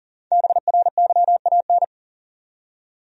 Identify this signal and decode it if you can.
Morse code, heard as a keyed tone, about 40 wpm, 700 Hz. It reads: BRYAN
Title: USA Common Names - 40wpm